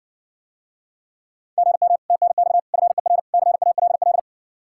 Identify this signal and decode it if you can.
Morse code, heard as a keyed tone, about 60 wpm, 700 Hz. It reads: GM TNX FER CALL